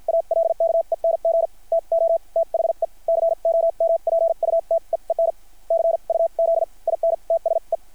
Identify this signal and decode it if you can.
Morse code, heard as a keyed tone, about 50 wpm, spaced as for 42 wpm, 660 Hz. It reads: ALKENG TO THE COMPUTEEN CUZ IN TSE
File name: archived / CRSnet